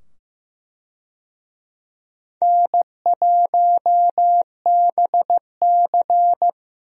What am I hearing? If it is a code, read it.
Morse code, heard as a keyed tone, about 15 wpm, 700 Hz. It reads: N1BC
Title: USA Call Signs - 4 Characters - 15wpm